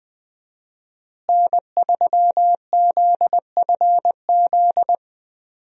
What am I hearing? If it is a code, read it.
Morse code, heard as a keyed tone, about 21 wpm, 700 Hz. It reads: N3ZFZ
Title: USA Call Signs - 5 Characters - Repeated 2x - 20wpm